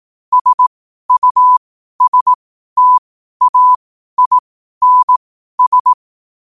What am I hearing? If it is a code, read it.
Morse code, heard as a keyed tone, about 18 wpm, spaced as for 12 wpm, 1000 Hz. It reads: SUSTAINS
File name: qotdfiles / 12wpm